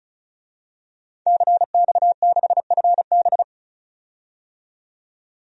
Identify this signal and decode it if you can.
Morse code, heard as a keyed tone, about 34 wpm, 700 Hz. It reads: CX6FB
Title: DXCC-100 Call Signs - 5 Characters - 35wpm